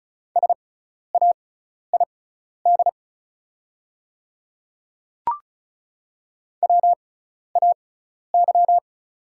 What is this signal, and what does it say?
Morse code, heard as a keyed tone, about 36 wpm, spaced as for 12 wpm, 700 Hz. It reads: SAID WAY
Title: 6x Character Spacing - Top 100 Words - 35wpm